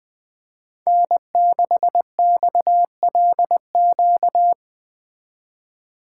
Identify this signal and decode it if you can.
Morse code, heard as a keyed tone, about 20 wpm, 700 Hz. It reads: N6XLQ
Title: Speed Racing - USA 1x3 Call Signs 17wpm